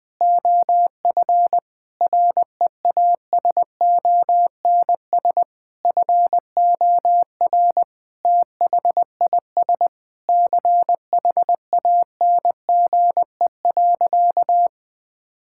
Morse code, heard as a keyed tone, about 20 wpm, 700 Hz. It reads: OF REASONS FOR THIS CHANGE.
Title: Sentences with 8-Word Phrases - 20wpm